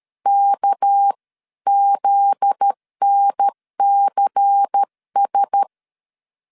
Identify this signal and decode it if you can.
Morse code, heard as a keyed tone, about 13 wpm, 800 Hz. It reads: KZNCS
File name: LEZIONI / 7 GF9Z / 7 GF9Z 12wpm 2147